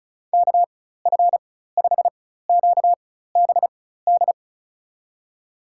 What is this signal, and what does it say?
Morse code, heard as a keyed tone, about 36 wpm, spaced as for 16 wpm, 700 Hz. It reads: KF5QBD
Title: USA Call Signs - 4x Character Spacing - 35wpm